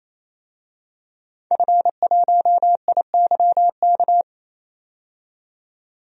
Morse code, heard as a keyed tone, about 28 wpm, 700 Hz. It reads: F1IYK
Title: DXCC-100 Call Signs - 5 Characters - 28wpm